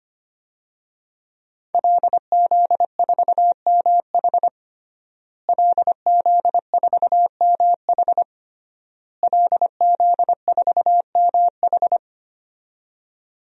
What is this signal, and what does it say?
Morse code, heard as a keyed tone, about 25 wpm, 700 Hz. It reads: LZ4MH LZ4MH LZ4MH
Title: DXCC-100 Call Signs - 5 Characters - Repeated 3x - 25wpm